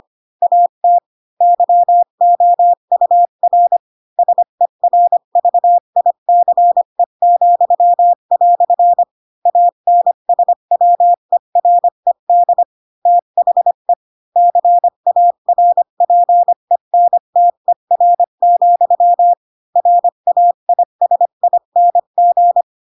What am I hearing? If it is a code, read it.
Morse code, heard as a keyed tone, about 25 wpm, 700 Hz. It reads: AT YOUR SERVICE," ANSWERED THE CARPENTER, RAISING